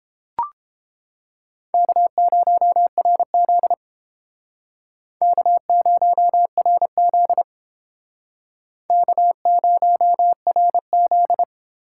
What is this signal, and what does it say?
Morse code, heard as a keyed tone, about 31 wpm, 700 Hz. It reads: K0RZ K0RZ K0RZ